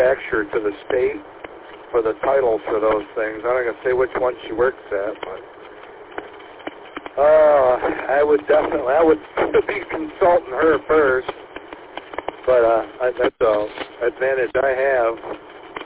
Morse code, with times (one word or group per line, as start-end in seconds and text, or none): none